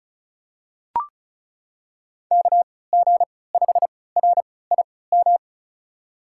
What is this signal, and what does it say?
Morse code, heard as a keyed tone, about 36 wpm, spaced as for 20 wpm, 700 Hz. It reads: KG5RIM
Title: USA Call Signs - 6 Characters - 3x Character Spacing - 35wpm